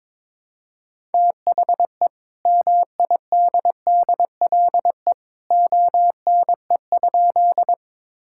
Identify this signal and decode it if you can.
Morse code, heard as a keyed tone, about 21 wpm, 700 Hz. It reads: THE MIDDLE ONE?
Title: Sentences from Top 1000 Words - Review Part 6 - 22wpm